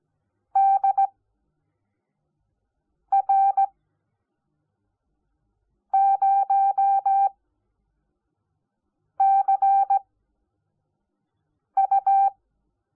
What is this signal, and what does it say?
Morse code, heard as a keyed tone, about 17 wpm, spaced as for 4 wpm, 780 Hz. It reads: DR0CU
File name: Onlinelehrgang / morsen / DJ4UF Morsen05